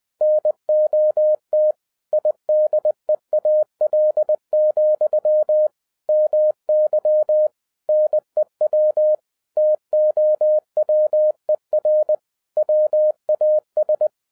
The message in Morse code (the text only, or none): NOT IDEAL, MY NEW TOWER WAS